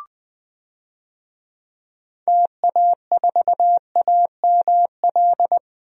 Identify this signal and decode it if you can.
Morse code, heard as a keyed tone, about 20 wpm, 700 Hz. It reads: TA4AML
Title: DXCC-100 Call Signs - 6 Characters - Repeated 3x - 20wpm